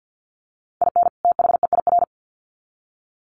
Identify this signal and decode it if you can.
Morse code, heard as a keyed tone, about 50 wpm, 700 Hz. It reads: IN THEIR
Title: Sets of 2 Words 50wpm